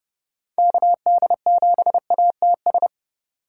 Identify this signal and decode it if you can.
Morse code, heard as a keyed tone, about 32 wpm, 700 Hz. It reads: KD7ATS